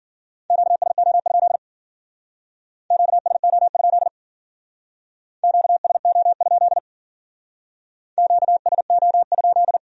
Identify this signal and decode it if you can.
Morse code, heard as a keyed tone, about 50 wpm, 700 Hz. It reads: QSO? QSO? QSO? QSO?
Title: Speed-Racing Q-Codes - 40wpm